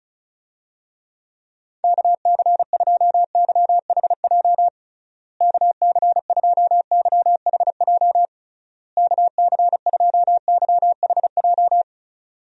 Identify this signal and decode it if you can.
Morse code, heard as a keyed tone, about 34 wpm, 700 Hz. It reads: KC2YHJ KC2YHJ KC2YHJ
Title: USA Call Signs - 6 Characters - Repeated 3x - 35wpm